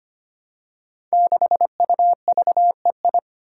Morse code, heard as a keyed tone, about 25 wpm, 700 Hz. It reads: 6UVEI
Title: Five Challenging Letter-Number Combinations 25wpm